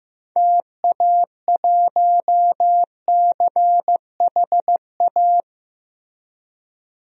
Morse code, heard as a keyed tone, about 15 wpm, 700 Hz. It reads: TA1CHA
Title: DXCC-100 Call Signs - 6 Characters - 15wpm